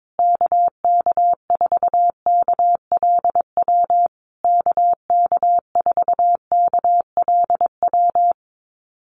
Morse code, heard as a keyed tone, about 22 wpm, 700 Hz. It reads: KK4KLW KK4KLW
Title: USA Call Signs Repeated 3x 22wpm